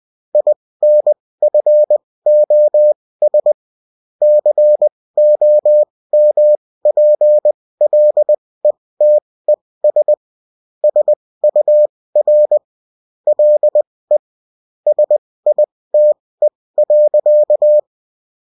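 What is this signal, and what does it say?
Morse code, heard as a keyed tone, about 20 wpm, spaced as for 16 wpm, 600 Hz. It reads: INFOS COMPLETES SUR LE SITE.
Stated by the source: Morse code bulletin of a French amateur CW union, sent as text and in CW.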